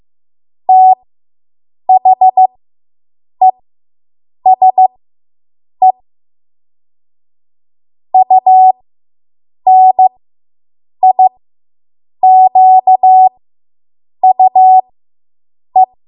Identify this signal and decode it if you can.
Morse code, heard as a keyed tone, about 15 wpm, spaced as for 7 wpm, 750 Hz. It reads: THESE UNIQUE